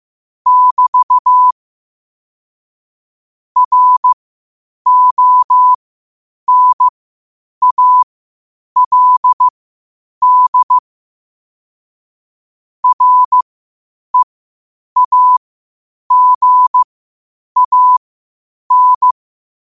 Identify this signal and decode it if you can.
Morse code, heard as a keyed tone, about 15 wpm, spaced as for 8 wpm, 1000 Hz. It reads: <BT> RONALD REAGAN
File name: qotdfiles / 8wpm